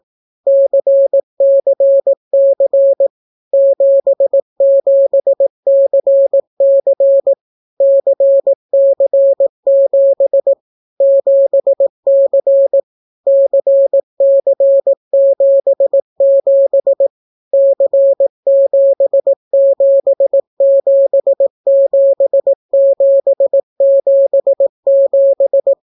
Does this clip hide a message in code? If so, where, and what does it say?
Yes: Morse code from 0.5 to 25.7 s, heard as a keyed tone, about 18 wpm, 550 Hz. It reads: CCC 77CC CC7 7C CC77 C7777777